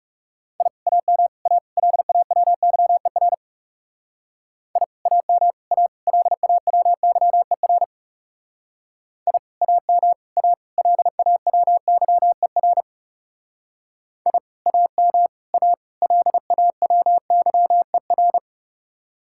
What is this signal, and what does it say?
Morse code, heard as a keyed tone, about 39 wpm, 700 Hz. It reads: I AM A LAWYER I AM A LAWYER I AM A LAWYER I AM A LAWYER